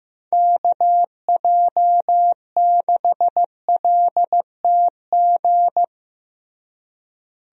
Morse code, heard as a keyed tone, about 15 wpm, 700 Hz. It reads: KJ6LTG
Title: USA Call Signs - 6 Characters - Repeated 2x - 15wpm